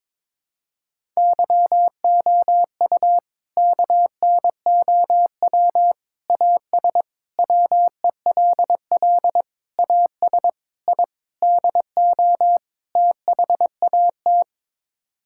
Morse code, heard as a keyed tone, about 22 wpm, 700 Hz. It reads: YOU KNOW AS WELL AS I DO THAT
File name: practice / 22wpm / Sets of 8 Words - 22wpm